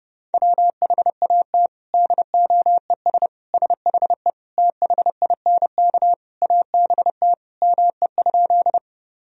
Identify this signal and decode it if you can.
Morse code, heard as a keyed tone, about 30 wpm, 700 Hz. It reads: WHAT DOES SHE THINK ABT ME?
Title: Sentences from Top 200 Words and Abbreviations 30wpm